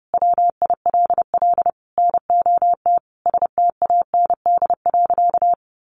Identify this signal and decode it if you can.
Morse code, heard as a keyed tone, about 30 wpm, 700 Hz. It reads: WILL NOT STAND.
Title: Sentences from Top 1300 Words 30wpm